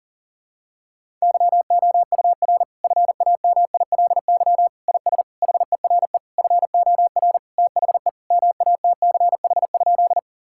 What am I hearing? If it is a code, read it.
Morse code, heard as a keyed tone, about 37 wpm, 700 Hz. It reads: YOUR FAMILY IS HERE FOR THE MATCH?
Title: Sentences from Top 1600 Words 40wpm